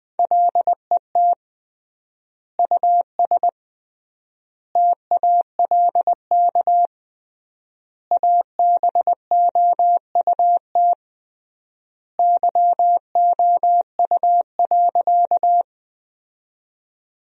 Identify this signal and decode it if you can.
Morse code, heard as a keyed tone, about 20 wpm, 700 Hz. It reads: LET US TALK ABOUT YOU.
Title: 3x Word-Spacing - Sentences from Top 400 Words - 20wpm